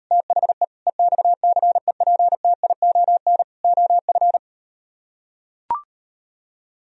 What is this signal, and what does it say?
Morse code, heard as a keyed tone, about 38 wpm, 700 Hz. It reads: THE EXCEPTION OF